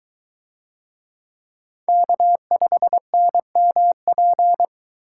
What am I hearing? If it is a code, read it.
Morse code, heard as a keyed tone, about 22 wpm, 700 Hz. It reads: K5NMP